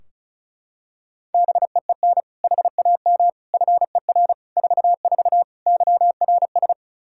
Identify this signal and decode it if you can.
Morse code, heard as a keyed tone, about 36 wpm, 700 Hz. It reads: BEEN HAM FER 44 YRS
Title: Practice QSO - Exchange 3 - More Info - 35wpm